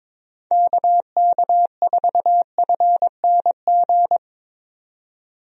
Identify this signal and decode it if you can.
Morse code, heard as a keyed tone, about 22 wpm, 700 Hz. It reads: KK4FNG
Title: USA Call Signs - 6 Characters - 22wpm